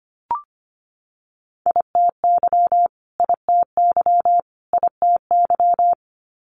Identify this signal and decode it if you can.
Morse code, heard as a keyed tone, about 25 wpm, 700 Hz. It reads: ITY ITY ITY
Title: Multi-letter Phonemes Repeated 3x 25wpm